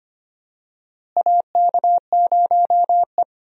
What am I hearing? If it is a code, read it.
Morse code, heard as a keyed tone, about 25 wpm, 700 Hz. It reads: AK0E
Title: USA Call Signs - 4 Characters - Repeated 3x - 25wpm